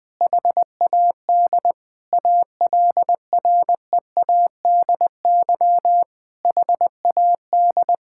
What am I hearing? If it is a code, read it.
Morse code, heard as a keyed tone, about 20 wpm, 700 Hz. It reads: HAD ALREADY HAD
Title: Sentences from Top 1600 Words 20wpm